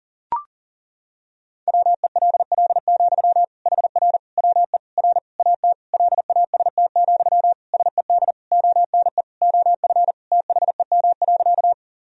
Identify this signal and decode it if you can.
Morse code, heard as a keyed tone, about 40 wpm, 700 Hz. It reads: WELL, HR WE R AT LAST, SED ONE OF THEM.
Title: Sentences from Top 300 Words and Abbreviations 40wpm